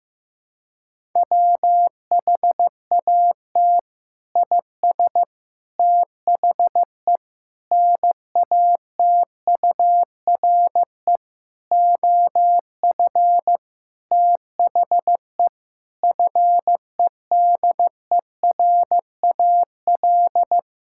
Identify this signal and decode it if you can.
Morse code, heard as a keyed tone, about 15 wpm, 700 Hz. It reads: WHAT IS THE NATURE OF THE FEDERAL
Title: Sentences from Top 1000 Words - Review Part 5 - 15wpm